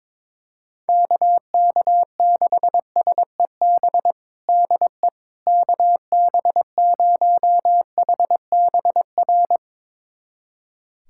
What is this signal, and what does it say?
Morse code, heard as a keyed tone, about 22 wpm, 700 Hz. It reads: KK6SEB DE KB0HBR
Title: USA Callsign DE Callsign - 6 Characters - 22wpm